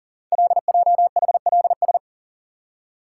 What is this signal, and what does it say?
Morse code, heard as a keyed tone, about 40 wpm, 700 Hz. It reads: LJHLS
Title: Code Groups - 5 Letters - 40wpm